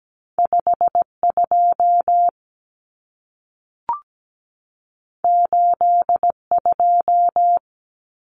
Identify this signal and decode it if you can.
Morse code, heard as a keyed tone, about 17 wpm, 700 Hz. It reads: 52 82
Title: Common Cut Numbers - 2-Digit 17wpm